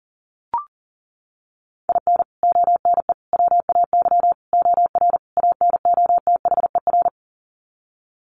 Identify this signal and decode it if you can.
Morse code, heard as a keyed tone, about 40 wpm, 700 Hz. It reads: IN ONE WAY OR ANOTHER
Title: Sets of 5 Words Encore 40wpm